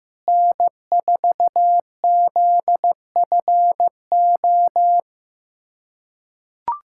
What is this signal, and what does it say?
Morse code, heard as a keyed tone, about 15 wpm, 700 Hz. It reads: N4ZFO
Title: USA Callsign DE Callsign - 5 Characters - 15wpm